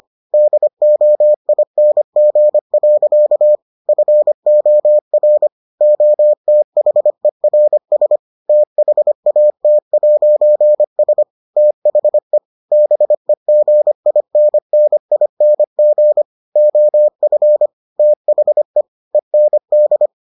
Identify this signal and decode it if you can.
Morse code, heard as a keyed tone, about 25 wpm, 600 Hz. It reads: DOING. FOR OTHERS THAT'S THE BEGINNING OF THE END